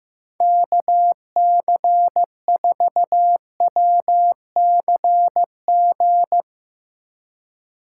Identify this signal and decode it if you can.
Morse code, heard as a keyed tone, about 16 wpm, 700 Hz. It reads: KC4WCG
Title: Speed Racing - USA 2x3 Call Signs 15wpm